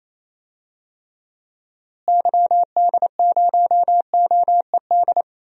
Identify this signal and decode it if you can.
Morse code, heard as a keyed tone, about 27 wpm, 700 Hz. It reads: YD0OED
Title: DXCC-100 Call Signs - 6 Characters - Repeated 3x - 28wpm